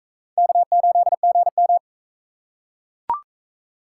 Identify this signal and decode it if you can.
Morse code, heard as a keyed tone, about 38 wpm, 700 Hz. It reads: K8GM